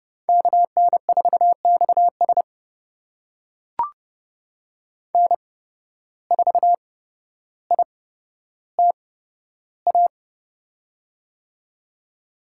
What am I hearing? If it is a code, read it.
Morse code, heard as a keyed tone, about 30 wpm, spaced as for 13 wpm, 700 Hz. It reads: KN4XS N4ITA
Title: USA Call Signs - 5 Characters - 8x Character Spacing - 30wpm